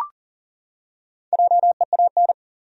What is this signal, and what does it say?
Morse code, heard as a keyed tone, about 40 wpm, 700 Hz. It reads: JEAN
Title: USA Common Names - Max 5 Letters - 40wpm